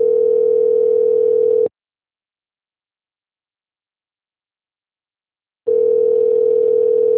Ringback tone (North America), 440 + 480 Hz.